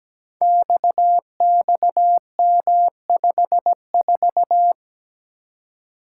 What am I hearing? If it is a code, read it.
Morse code, heard as a keyed tone, about 17 wpm, 700 Hz. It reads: XXM54
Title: Code Groups - 5 Letter:Numbers - 17wpm